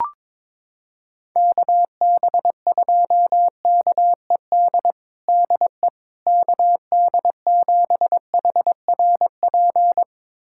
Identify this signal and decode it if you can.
Morse code, heard as a keyed tone, about 22 wpm, 700 Hz. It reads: KB2KED DE KD7HRP
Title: USA Callsign DE Callsign 22wpm